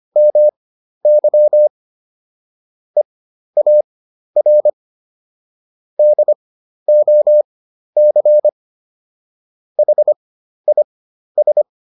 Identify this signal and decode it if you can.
Morse code, heard as a keyed tone, about 25 wpm, spaced as for 12 wpm, 600 Hz. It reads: MY EAR DOC HIS